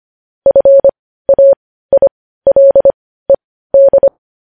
Morse code, heard as a keyed tone, about 25 wpm, spaced as for 15 wpm, 550 Hz. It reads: FAILED
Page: news headlines in morse code at 15 wpm